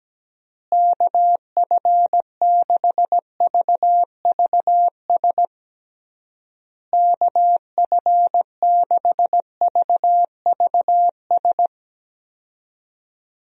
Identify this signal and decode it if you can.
Morse code, heard as a keyed tone, about 17 wpm, 700 Hz. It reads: KF6VVS KF6VVS
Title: USA Call Signs - 6 Characters - Repeated 3x - 17wpm